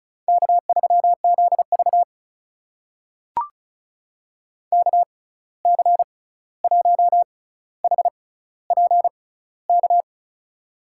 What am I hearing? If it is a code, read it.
Morse code, heard as a keyed tone, about 36 wpm, spaced as for 16 wpm, 700 Hz. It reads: K3ZV KC1HPK
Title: USA Call Signs - 6x Character Spacing - 35wpm